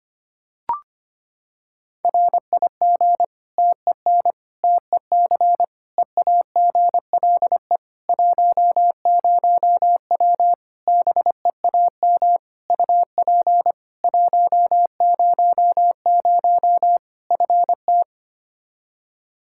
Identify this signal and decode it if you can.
Morse code, heard as a keyed tone, about 25 wpm, 700 Hz. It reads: RIG TEN TEC EAGLE 10W BEAM UP 100 FT
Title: Practice QSO - Exchange 2 - Info - 25wpm